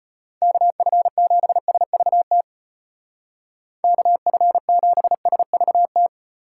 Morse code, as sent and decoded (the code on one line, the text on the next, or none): -.- ..-. --... ... ...- - / -.- ..-. --... ... ...- -
KF7SVT KF7SVT